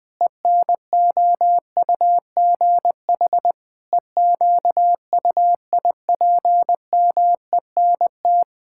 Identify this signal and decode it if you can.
Morse code, heard as a keyed tone, about 21 wpm, 700 Hz. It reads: ENOUGH EQUIPMENT
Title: Sentences from Top 1000 Words - Review Part 5 - 20wpm